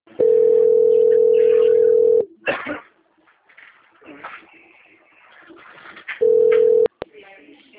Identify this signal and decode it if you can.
Ringback tone (North America), 440 + 480 Hz.